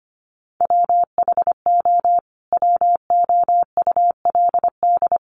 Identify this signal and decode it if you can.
Morse code, heard as a keyed tone, about 25 wpm, 700 Hz. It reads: WHO WOULD
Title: Sentences from Top 1200 Words 25wpm